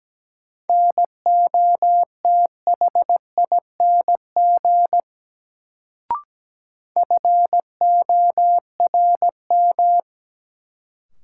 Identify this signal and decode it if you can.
Morse code, heard as a keyed tone, about 17 wpm, 700 Hz. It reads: NOTHING FORM
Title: Sets of 4 Words - New Word Drill - 17wpm